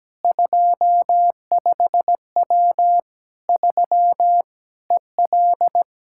Morse code, heard as a keyed tone, about 17 wpm, 700 Hz. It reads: 25W 3 EL
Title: Practice QSO - Exchange 2 - Info - 17wpm